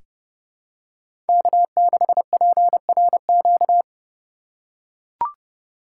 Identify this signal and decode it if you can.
Morse code, heard as a keyed tone, about 30 wpm, 700 Hz. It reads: K6PRQ
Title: USA Call Signs - 5 Characters - 3x Character Spacing - 30wpm